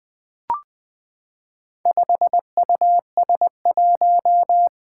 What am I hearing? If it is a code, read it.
Morse code, heard as a keyed tone, about 20 wpm, 700 Hz. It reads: 5US1
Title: Four Challenging Letter-Number Combinations 20wpm